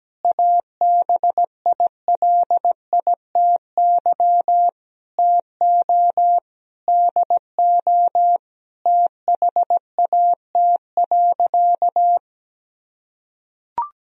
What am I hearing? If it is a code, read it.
Morse code, heard as a keyed tone, about 17 wpm, 700 Hz. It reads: ABILITY TO DO THAT.
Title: Sentences from Top 1000 Words - Review Part 3 - 17wpm